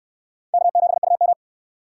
Morse code, heard as a keyed tone, about 68 wpm, 700 Hz. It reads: K7UM